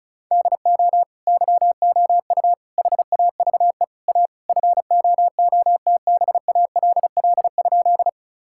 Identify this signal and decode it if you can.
Morse code, heard as a keyed tone, about 36 wpm, 700 Hz. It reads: DO YOU HAVE A FOOTBALL?
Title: Sentences from Top 1600 Words 35wpm